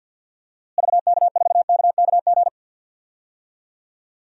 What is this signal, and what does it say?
Morse code, heard as a keyed tone, about 50 wpm, 700 Hz. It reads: VK4KKG
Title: DXCC-100 Call Signs - 6 Characters - Repeated 2x - 50wpm